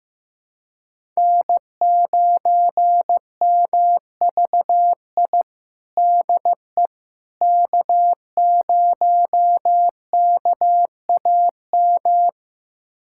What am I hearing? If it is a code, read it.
Morse code, heard as a keyed tone, about 15 wpm, 700 Hz. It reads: N9MVI DE K0KAM
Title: USA Callsign DE Callsign - 5 Characters - 15wpm